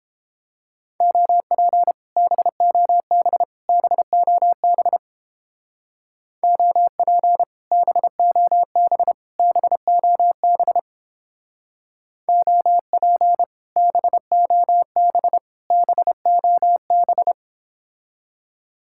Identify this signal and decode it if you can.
Morse code, heard as a keyed tone, about 30 wpm, 700 Hz. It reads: OP BOB BOB OP BOB BOB OP BOB BOB